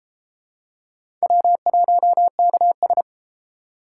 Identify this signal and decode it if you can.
Morse code, heard as a keyed tone, about 31 wpm, 700 Hz. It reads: W1KS